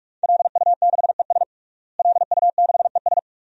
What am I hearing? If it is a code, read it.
Morse code, heard as a keyed tone, about 48 wpm, 700 Hz. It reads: LU6ES LU6ES
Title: DXCC-100 Call Signs - 5 Characters - Repeated 2x - 45wpm